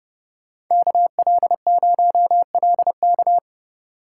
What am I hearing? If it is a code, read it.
Morse code, heard as a keyed tone, about 31 wpm, 700 Hz. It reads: KL0LK